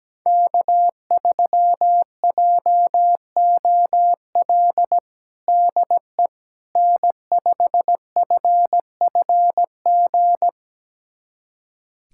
Morse code, heard as a keyed tone, about 17 wpm, 700 Hz. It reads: K3JOL DE N5FFG
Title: USA Callsign DE Callsign - 5 Characters - 17wpm